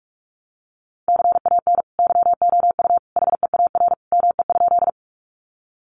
Morse code, heard as a keyed tone, about 44 wpm, 700 Hz. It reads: CAN YOU HEAR ME?
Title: Sentences from Top 300 Words 45wpm